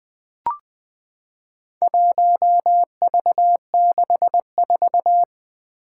Morse code, heard as a keyed tone, about 20 wpm, 700 Hz. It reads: 1V64
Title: Four Challenging Letter-Number Combinations 20wpm